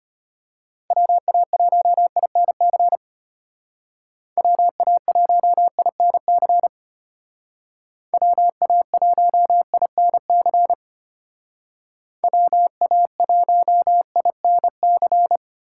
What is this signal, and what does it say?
Morse code, heard as a keyed tone, about 31 wpm, 700 Hz. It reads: WA1INC WA1INC WA1INC WA1INC
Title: Speed Racing - USA 2x3 Call Signs 25wpm